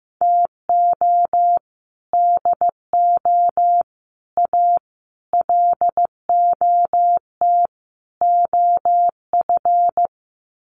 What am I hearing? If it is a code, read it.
Morse code, heard as a keyed tone, about 15 wpm, 700 Hz. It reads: TO DO A LOT OF
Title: Sets of 5 Words Encore 15wpm